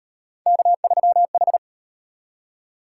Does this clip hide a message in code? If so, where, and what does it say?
Yes: Morse code from 0.5 to 1.6 s, heard as a keyed tone, about 38 wpm, 700 Hz. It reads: K3H